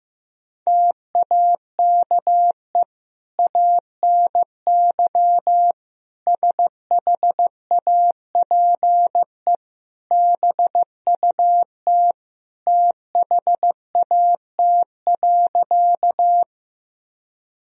Morse code, heard as a keyed tone, about 15 wpm, 700 Hz. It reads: TAKE ANY SHAPE BUT THAT.
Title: Sentences from Top 1000 Words - Review Part 3 - 15wpm